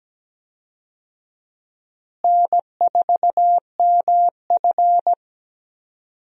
Morse code, heard as a keyed tone, about 17 wpm, 700 Hz. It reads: N4MF
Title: USA Call Signs - 4 Characters - Repeated 3x - 17wpm